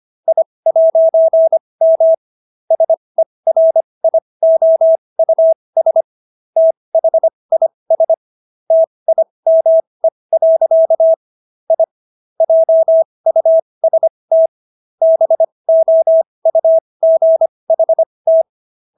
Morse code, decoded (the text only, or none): I'M SERIOUS THIS TIME. I JUST BOUGHT